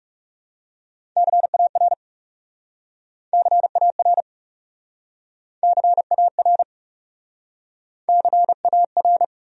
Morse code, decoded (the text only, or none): CAR CAR CAR CAR